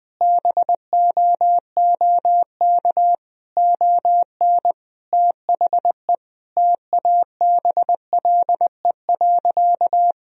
Morse code, heard as a keyed tone, about 20 wpm, 700 Hz. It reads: BOOK ON THE TABLE.
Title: Sentences from Top 1000 Words - Review Part 2 - 20wpm